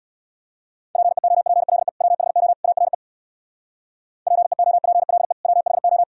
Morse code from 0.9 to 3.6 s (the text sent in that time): PEOPLE WHO ARE